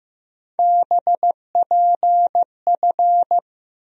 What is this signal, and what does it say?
Morse code, heard as a keyed tone, about 15 wpm, 700 Hz. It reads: BPF
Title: USA Call Sign Postfixes - 3 Letters 15wpm